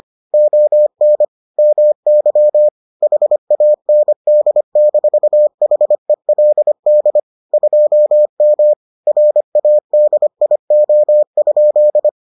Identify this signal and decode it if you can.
Morse code, heard as a keyed tone, about 25 wpm, 600 Hz. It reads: ON MY HAND-HELD 2M RADIO?